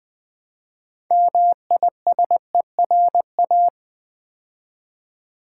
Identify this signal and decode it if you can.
Morse code, heard as a keyed tone, about 20 wpm, 700 Hz. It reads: MISERA
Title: Build A Word - 3 Letters At A Time - 9-Letter Words - 20wpm